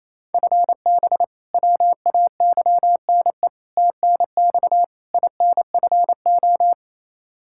Morse code, heard as a keyed tone, about 28 wpm, 700 Hz. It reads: FB WAYNE TNX INFO
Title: K1USN SST QSO Practice 28wpm